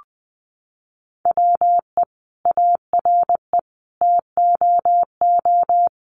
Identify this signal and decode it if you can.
Morse code, heard as a keyed tone, about 20 wpm, 700 Hz. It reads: WE ARE TOO
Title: Sentences from Top 1100 Words 20wpm